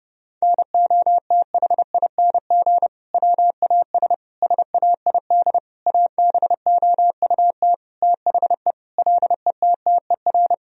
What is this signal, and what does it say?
Morse code, heard as a keyed tone, about 32 wpm, 700 Hz. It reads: NOTHING WAS SAID ABOUT THE LETTER
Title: Sentences from Top 1000 Words - Review Part 2 - 30wpm